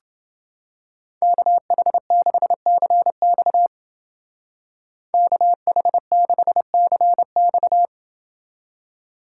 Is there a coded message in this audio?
Yes: Morse code from 1.2 to 7.9 s, heard as a keyed tone, about 29 wpm, 700 Hz. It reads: KH6CX KH6CX